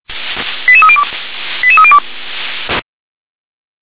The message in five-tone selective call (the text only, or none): 88302 88302